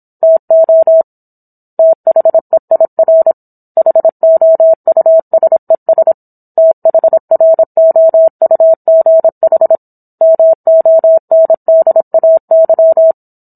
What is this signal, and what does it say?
Morse code, heard as a keyed tone, about 27 wpm, 650 Hz. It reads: TO THEIR HOUSES THROUGH MONDAY
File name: Book One / 17-Chapter 17 - 26 WPM